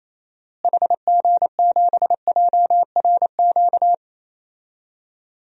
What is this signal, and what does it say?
Morse code, heard as a keyed tone, about 27 wpm, 700 Hz. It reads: HG7JRQ
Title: DXCC-100 Call Signs - 6 Characters - Repeated 2x - 28wpm